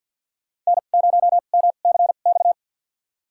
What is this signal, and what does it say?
Morse code, heard as a keyed tone, about 50 wpm, spaced as for 36 wpm, 700 Hz. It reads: N0MCX
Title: USA Call Signs - 5 Characters - 2x Character Spacing - 50wpm